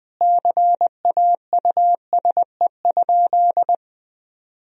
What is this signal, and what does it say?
Morse code, heard as a keyed tone, about 21 wpm, 700 Hz. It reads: CAUSE?
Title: Sentences from Top 1000 Words - Review Part 4 - 20wpm